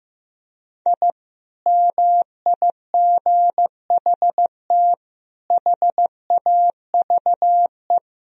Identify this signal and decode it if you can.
Morse code, heard as a keyed tone, about 16 wpm, 700 Hz. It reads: I MIGHT HAVE
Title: Sentences from Top 1000 Words - Review Part 3 - 15wpm